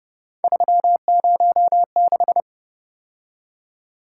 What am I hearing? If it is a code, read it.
Morse code, heard as a keyed tone, about 28 wpm, 700 Hz. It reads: 306